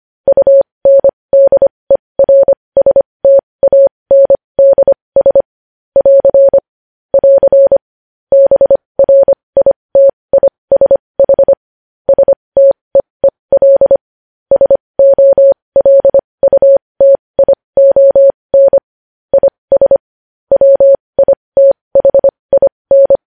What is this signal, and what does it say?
Morse code, heard as a keyed tone, about 25 wpm, spaced as for 20 wpm, 550 Hz. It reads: UNDERSTANDS <AR> <AR> BRITISH STEEL SOLUTION IS WITHIN
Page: news headlines in morse code at 20 wpm